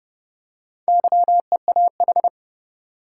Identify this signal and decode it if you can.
Morse code, heard as a keyed tone, about 30 wpm, 700 Hz. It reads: YEAH
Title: Four Letter Words - 30wpm